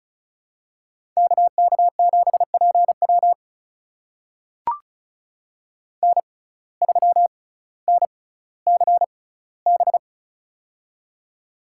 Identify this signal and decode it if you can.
Morse code, heard as a keyed tone, about 34 wpm, spaced as for 18 wpm, 700 Hz. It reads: KK7PW N3NCB
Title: USA Call Signs - 5 Characters - 6x Character Spacing - 35wpm